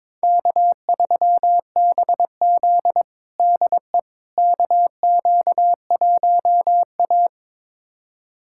Morse code, heard as a keyed tone, about 22 wpm, 700 Hz. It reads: K3BZ DE KQ1A
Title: USA Callsign DE Callsign - 4 Characters - 22wpm